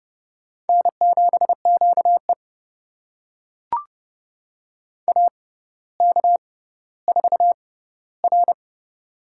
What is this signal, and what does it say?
Morse code, heard as a keyed tone, about 30 wpm, spaced as for 15 wpm, 690 Hz. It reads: N7QE AK4R
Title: USA Call Signs - 4 Characters - 6x Character Spacing - 30wpm